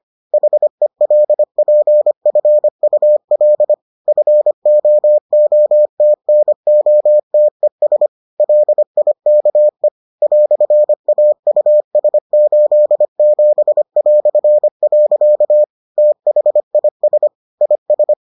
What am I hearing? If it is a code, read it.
Morse code, heard as a keyed tone, about 25 wpm, 600 Hz. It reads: HELPFUL FOOTNOTES LIKE "AUS87". THIS IS